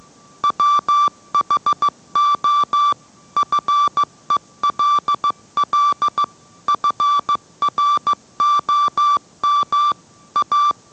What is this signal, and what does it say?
Morse code, heard as a keyed tone, about 15 wpm, 1170 Hz. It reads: WHO FELL FROM A